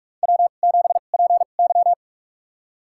Morse code, heard as a keyed tone, about 47 wpm, spaced as for 33 wpm, 700 Hz. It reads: W7PY
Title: USA Call Signs - 2x Character Spacing - 45wpm